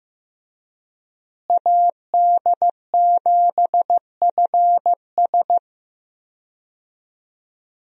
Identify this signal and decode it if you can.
Morse code, heard as a keyed tone, about 15 wpm, 700 Hz. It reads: AD7FS
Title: USA Call Signs - 5 Characters - Repeated 2x - 15wpm